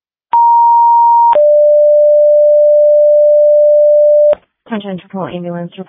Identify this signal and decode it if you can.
Two-tone page, heard as a one-second tone, then a three-second tone, about 940 then 590 Hz, with speech. It is heard from 0.3 to 4.4 s.